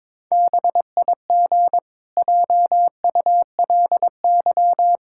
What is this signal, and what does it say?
Morse code, heard as a keyed tone, about 22 wpm, 700 Hz. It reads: BIG JULY
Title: Pangram Challenge v2 - 22wpm